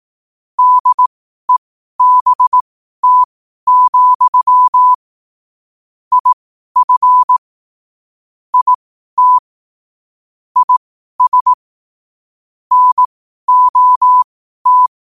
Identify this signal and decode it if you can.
Morse code, heard as a keyed tone, about 18 wpm, spaced as for 12 wpm, 1000 Hz. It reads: DEBT, IF IT IS NOT
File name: qotdfiles / 12wpm